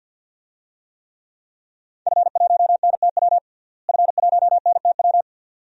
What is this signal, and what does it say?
Morse code, heard as a keyed tone, about 52 wpm, 700 Hz. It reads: F1NTW F1NTW
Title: DXCC-100 Call Signs - 5 Characters - Repeated 2x - 50wpm